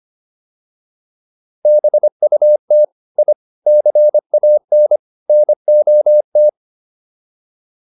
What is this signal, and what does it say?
Morse code, heard as a keyed tone, about 25 wpm, 600 Hz. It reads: BUT I CAN NOT